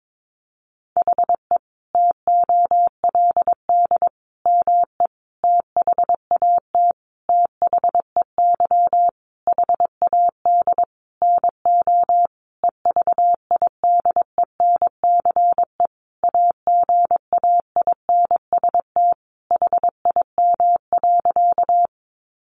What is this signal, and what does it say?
Morse code, heard as a keyed tone, about 22 wpm, 700 Hz. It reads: HE TOLD ME THAT THEY HAD NO EVIDENCE AGAINST HIM.
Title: Sentences from Top 1300 Words 22wpm